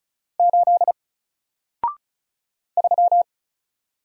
Morse code, heard as a keyed tone, about 34 wpm, 700 Hz. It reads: 8 3
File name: Single-Letter-Number-35wpm